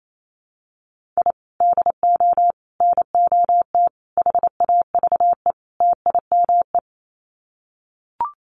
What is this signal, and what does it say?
Morse code, heard as a keyed tone, about 28 wpm, 700 Hz. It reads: I DO NOT HAVE TIME
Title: Sets of 5 Words Encore 28wpm